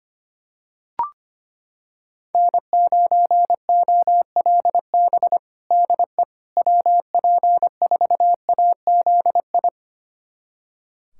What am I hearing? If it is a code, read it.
Morse code, heard as a keyed tone, about 25 wpm, 700 Hz. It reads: N9OLB DE WP4AZI
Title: USA Callsign DE Callsign 25wpm